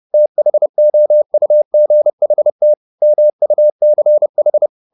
Morse code, heard as a keyed tone, about 30 wpm, 600 Hz. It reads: THOUGHT MUCH